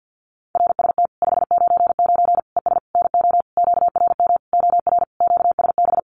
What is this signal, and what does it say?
Morse code, heard as a keyed tone, about 50 wpm, 700 Hz. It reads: RST 599 ES NO QRM OR QSB
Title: Common QSO Phrases 50wpm